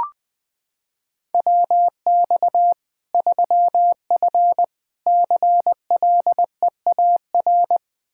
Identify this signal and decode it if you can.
Morse code, heard as a keyed tone, about 21 wpm, 700 Hz. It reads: WX 3F CLEAR
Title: K1USN SST QSO Practice 20wpm